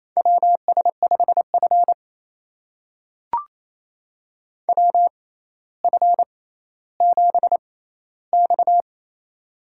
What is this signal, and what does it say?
Morse code, heard as a keyed tone, about 28 wpm, spaced as for 14 wpm, 700 Hz. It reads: WS5F WF7X
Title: USA Call Signs - 4 Characters - 6x Character Spacing - 28wpm